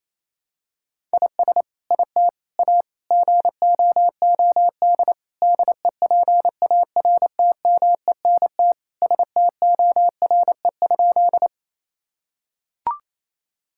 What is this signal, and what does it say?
Morse code, heard as a keyed tone, about 28 wpm, 700 Hz. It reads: IS IT A GOOD DEPARTMENT STORE?
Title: Sentences from Top 1000 Words - Review Part 5 - 28wpm